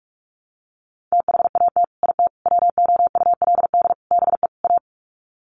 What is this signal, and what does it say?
Morse code, heard as a keyed tone, about 44 wpm, 700 Hz. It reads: THAT IT WOULD BE A
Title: Sets of 5 Words Encore2 45wpm